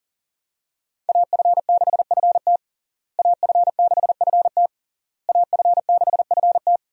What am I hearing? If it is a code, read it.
Morse code, heard as a keyed tone, about 40 wpm, 690 Hz. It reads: AF6FT AF6FT AF6FT
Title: USA Call Signs - 5 Characters - Repeated 3x - 40wpm